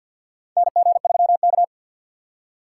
Morse code, heard as a keyed tone, about 52 wpm, 700 Hz. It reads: NG3K
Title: USA Call Signs - 4 Characters - 2x Character Spacing - 50wpm